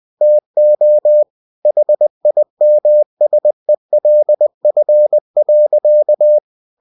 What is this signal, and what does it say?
Morse code, heard as a keyed tone, about 20 wpm, 600 Hz. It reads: TO HIMSELF.